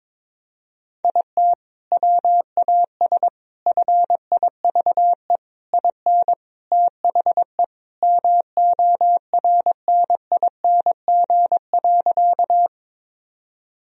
Morse code, heard as a keyed tone, about 22 wpm, 700 Hz. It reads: IT WAS FIVE IN THE MORNING.
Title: Sentences from Top 1000 Words - Review Part 2 - 22wpm